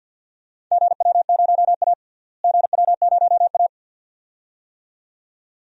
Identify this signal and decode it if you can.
Morse code, heard as a keyed tone, about 51 wpm, 700 Hz. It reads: GW0A GW0A
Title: DXCC-100 Call Signs - 4 Characters - Repeated 2x - 50wpm